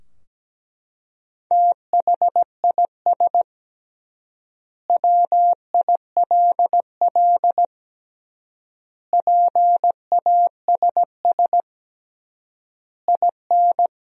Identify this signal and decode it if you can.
Morse code, heard as a keyed tone, about 17 wpm, 700 Hz. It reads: THIS WILL PASS IN
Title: 3x Word-Spacing - Sentences from Top 400 Words - 17wpm